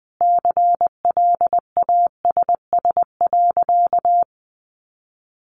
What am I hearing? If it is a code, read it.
Morse code, heard as a keyed tone, about 20 wpm, 700 Hz. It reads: CLASS.
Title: Sentences from Top 1400 Words 20wpm